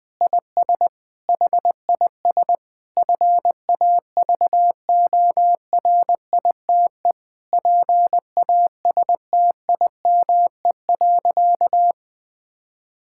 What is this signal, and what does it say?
Morse code, heard as a keyed tone, about 21 wpm, 700 Hz. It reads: IS HIS FAVORITE PASTIME.
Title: Difficult Sentences with Hints - 20wpm